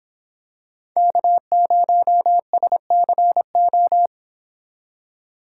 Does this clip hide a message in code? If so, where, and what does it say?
Yes: Morse code from 1.0 to 4.1 s, heard as a keyed tone, about 26 wpm, 700 Hz. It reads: K0SCO